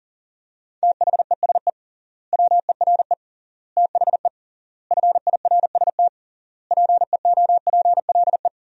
Morse code, heard as a keyed tone, about 40 wpm, 700 Hz. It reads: THESE WERE THE FIRST PEOPLE
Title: 3x Word-Spacing - Sentences from Top 300 Words - 40wpm